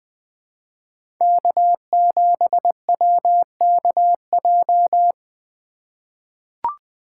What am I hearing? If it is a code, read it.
Morse code, heard as a keyed tone, about 20 wpm, 700 Hz. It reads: K7WKJ
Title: USA Call Signs - 2x Character Spacing - 20wpm